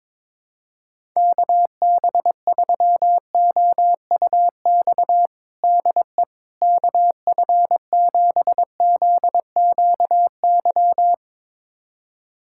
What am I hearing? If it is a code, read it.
Morse code, heard as a keyed tone, about 21 wpm, 700 Hz. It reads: KB3OUX DE KF7ZQY
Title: USA Callsign DE Callsign - 6 Characters - 22wpm